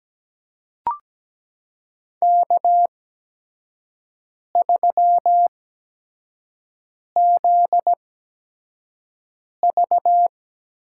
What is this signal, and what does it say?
Morse code, heard as a keyed tone, about 17 wpm, spaced as for 5 wpm, 700 Hz. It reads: K3ZV
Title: USA Call Signs - 8x Character Spacing - 17wpm